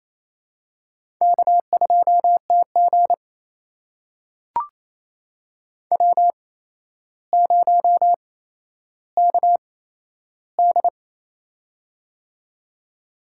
Morse code, heard as a keyed tone, about 28 wpm, spaced as for 12 wpm, 700 Hz. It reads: K2TG W0KD
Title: USA Call Signs - 8x Character Spacing - 28wpm